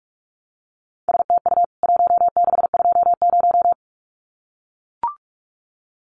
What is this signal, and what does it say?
Morse code, heard as a keyed tone, about 44 wpm, 700 Hz. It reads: STU 1620
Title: CWT Exchange 45wpm